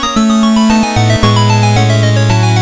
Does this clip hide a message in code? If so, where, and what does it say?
No.